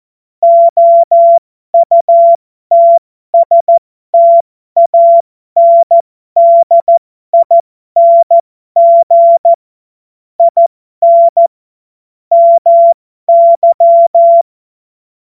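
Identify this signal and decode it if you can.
Morse code, heard as a keyed tone, about 14 wpm, spaced as for 12 wpm, 680 Hz. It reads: OUTSTANDING IN MY
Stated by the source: Morse code joke.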